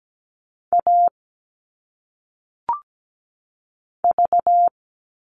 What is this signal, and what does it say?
Morse code, heard as a keyed tone, about 17 wpm, 700 Hz. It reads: A V
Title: All Cut Numbers - 1-Digit 17wpm